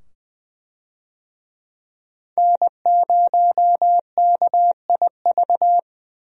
Morse code, heard as a keyed tone, about 20 wpm, 700 Hz. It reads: N0KIV
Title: USA Call Signs - 5 Characters - 20wpm